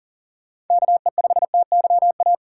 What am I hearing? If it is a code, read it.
Morse code, heard as a keyed tone, about 40 wpm, 700 Hz. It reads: KE5TYA
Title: USA Call Signs - 6 Characters - 6x Character Spacing - 40wpm